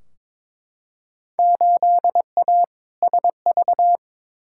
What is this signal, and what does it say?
Morse code, heard as a keyed tone, about 22 wpm, 700 Hz. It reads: 8A SV
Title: Field Day Exchanges Encore - 22wpm